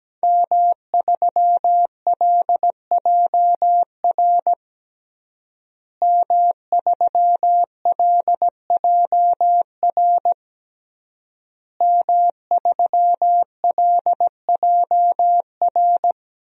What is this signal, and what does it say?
Morse code, heard as a keyed tone, about 17 wpm, 700 Hz. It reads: M3LJR M3LJR M3LJR
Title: DXCC-100 Call Signs - 5 Characters - Repeated 3x - 17wpm